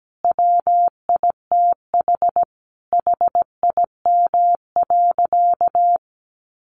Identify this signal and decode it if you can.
Morse code, heard as a keyed tone, about 17 wpm, 700 Hz. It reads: WITH HIM.
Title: Sentences from Top 1300 Words 17wpm